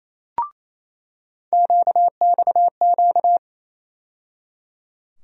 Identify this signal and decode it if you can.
Morse code, heard as a keyed tone, about 28 wpm, 700 Hz. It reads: QXQ
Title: USA Call Sign Postfixes - 3 Letters 28wpm